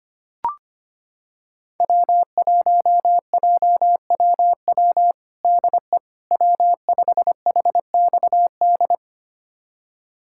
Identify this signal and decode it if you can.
Morse code, heard as a keyed tone, about 25 wpm, 700 Hz. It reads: W1JWW DE W5HXD
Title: USA Callsign DE Callsign 25wpm